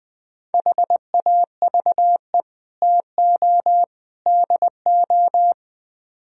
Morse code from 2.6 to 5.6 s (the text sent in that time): TO DO